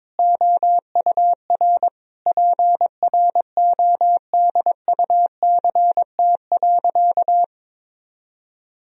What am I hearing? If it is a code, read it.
Morse code, heard as a keyed tone, about 22 wpm, 700 Hz. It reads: OUR PRODUCT.
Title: Sentences from Top 1000 Words - Review Part 3 - 22wpm